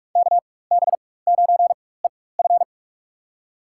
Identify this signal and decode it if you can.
Morse code, heard as a keyed tone, about 46 wpm, spaced as for 21 wpm, 700 Hz. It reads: KB9EF
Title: USA Call Signs - 5 Characters - 4x Character Spacing - 45wpm